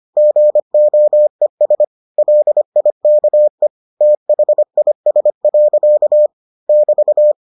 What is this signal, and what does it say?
Morse code, heard as a keyed tone, about 25 wpm, 600 Hz. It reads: GOES LIKE THIS. <BT>